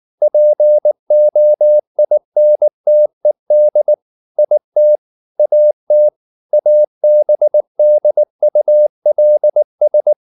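Morse code, heard as a keyed tone, about 19 wpm, 600 Hz. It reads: POINTED IT AT ABDULS
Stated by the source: Morse code at 19 wpm from CW med Gnister, reading Ken Follett's novel Never in English.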